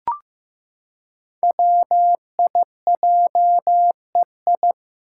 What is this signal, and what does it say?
Morse code, heard as a keyed tone, about 15 wpm, 700 Hz. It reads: WIJEI
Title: Five Challenging Letter-Number Combinations 15wpm